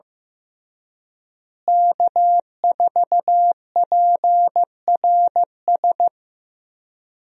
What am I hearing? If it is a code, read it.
Morse code, heard as a keyed tone, about 15 wpm, 700 Hz. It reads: K4PRS
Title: USA Call Signs - 5 Characters - Repeated 3x - 15wpm